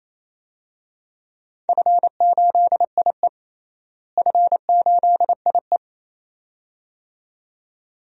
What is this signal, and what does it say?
Morse code, heard as a keyed tone, about 28 wpm, 700 Hz. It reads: F8IE F8IE
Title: DXCC-100 Call Signs - 4 Characters - Repeated 2x - 28wpm